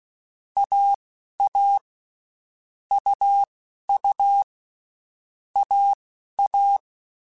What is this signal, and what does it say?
Morse code, heard as a keyed tone, about 16 wpm, spaced as for 11 wpm, 800 Hz. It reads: AA UU AA
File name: Basic Letter Groups By7Groups / AWUNDG / AWUNDG@10